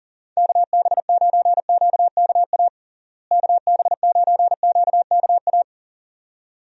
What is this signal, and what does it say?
Morse code, heard as a keyed tone, about 40 wpm, 700 Hz. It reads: KB9QKA KB9QKA
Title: USA Call Signs - 6 Characters - Repeated 2x - 40wpm